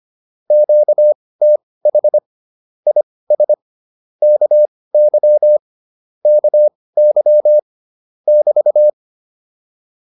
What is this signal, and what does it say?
Morse code, heard as a keyed tone, about 25 wpm, spaced as for 18 wpm, 600 Hz. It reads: QTH IS KY KY <BT>